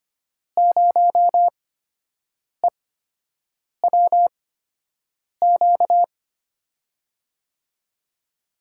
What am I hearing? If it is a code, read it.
Morse code, heard as a keyed tone, about 25 wpm, spaced as for 7 wpm, 700 Hz. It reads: 0EWQ
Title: USA Call Signs - 5 Characters - 8x Character Spacing - 25wpm